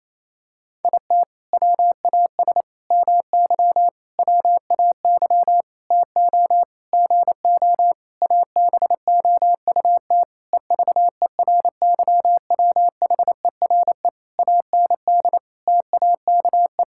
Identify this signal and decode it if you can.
Morse code, heard as a keyed tone, about 28 wpm, 700 Hz. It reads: IT WAS MY WAY TO GO ABOUT EVERYWHERE AND TAKE
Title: Sentences from Top 1600 Words 28wpm